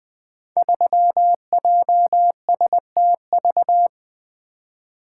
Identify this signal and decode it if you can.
Morse code, heard as a keyed tone, about 21 wpm, 700 Hz. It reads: 3JSTV
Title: Code Groups - 5 Letter:Numbers - 20wpm